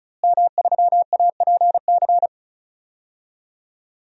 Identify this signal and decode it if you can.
Morse code, heard as a keyed tone, about 36 wpm, 700 Hz. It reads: M3APC
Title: DXCC-100 Call Signs - 5 Characters - Repeated 3x - 35wpm